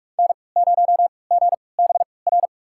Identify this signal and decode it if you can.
Morse code, heard as a keyed tone, about 42 wpm, spaced as for 25 wpm, 700 Hz. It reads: N0GBR
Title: USA Call Signs - 5 Characters - 3x Character Spacing - 45wpm